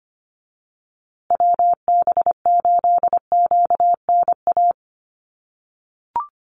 Morse code, heard as a keyed tone, about 25 wpm, 700 Hz. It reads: WB8QNA
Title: USA Call Signs Repeated 3x 25wpm